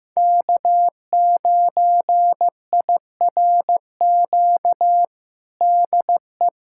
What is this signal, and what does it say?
Morse code, heard as a keyed tone, about 15 wpm, 700 Hz. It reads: K9IRQ DE
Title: USA Callsign DE Callsign - 5 Characters - 15wpm